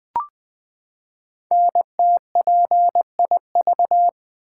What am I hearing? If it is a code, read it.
Morse code, heard as a keyed tone, about 20 wpm, 700 Hz. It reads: NTPIV
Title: Code Groups - 5 Letter:Numbers - 20wpm